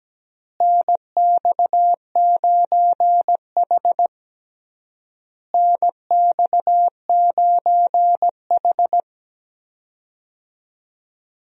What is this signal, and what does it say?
Morse code, heard as a keyed tone, about 17 wpm, 700 Hz. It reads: NX9H NX9H
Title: USA Call Signs - 4 Characters - Repeated 3x - 17wpm